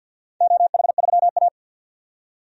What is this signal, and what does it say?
Morse code, heard as a keyed tone, about 48 wpm, 700 Hz. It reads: OH3A
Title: DXCC-100 Call Signs - 4 Characters - 50wpm